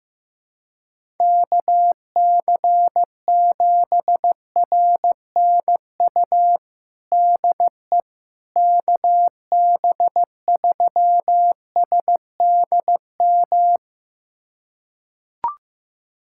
Morse code, heard as a keyed tone, about 15 wpm, 700 Hz. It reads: KC7RNU DE KB3SDM
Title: USA Callsign DE Callsign - 6 Characters - 15wpm